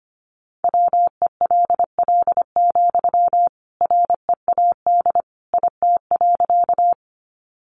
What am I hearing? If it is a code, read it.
Morse code, heard as a keyed tone, about 25 wpm, 700 Hz. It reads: WELL, READ IT.
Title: Sentences from Top 200 Words 25wpm